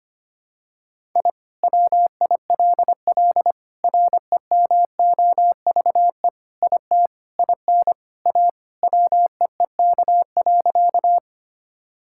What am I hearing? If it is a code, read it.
Morse code, heard as a keyed tone, about 25 wpm, 700 Hz. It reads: I WILL REMOVE IT IN A WEEK.
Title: Sentences from Top 1000 Words - Review Part 5 - 25wpm